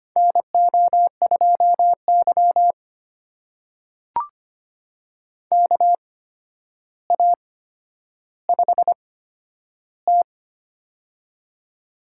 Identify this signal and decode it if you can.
Morse code, heard as a keyed tone, about 25 wpm, spaced as for 11 wpm, 700 Hz. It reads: NO2Y KA5T
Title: USA Call Signs - 4 Characters - 8x Character Spacing - 25wpm